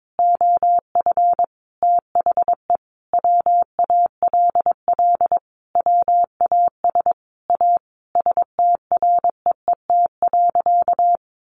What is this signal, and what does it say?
Morse code, heard as a keyed tone, about 22 wpm, 700 Hz. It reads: OF THE WALL WAS A STREET.
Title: Sentences from Top 1100 Words 22wpm